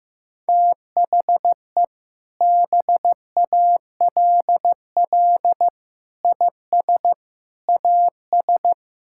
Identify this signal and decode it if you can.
Morse code, heard as a keyed tone, about 15 wpm, 700 Hz. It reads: THE BALL IS AS
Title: Sentences from Top 1000 Words - Review Part 6 - 15wpm